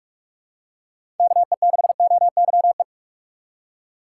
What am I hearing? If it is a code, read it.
Morse code, heard as a keyed tone, about 44 wpm, 700 Hz. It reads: KE6OYE
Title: USA Call Signs - 6 Characters - 6x Character Spacing - 45wpm